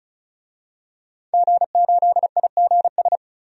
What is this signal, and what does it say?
Morse code, heard as a keyed tone, about 34 wpm, 700 Hz. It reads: G8IGS
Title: DXCC-100 Call Signs - 5 Characters - 35wpm